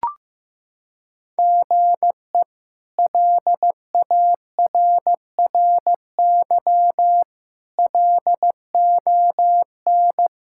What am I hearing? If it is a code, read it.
Morse code, heard as a keyed tone, about 15 wpm, 700 Hz. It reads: GE LARRY LON
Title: K1USN SST QSO Practice - 15wpm